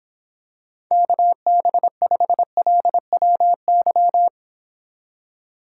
Morse code, heard as a keyed tone, about 26 wpm, 700 Hz. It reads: KB5LWY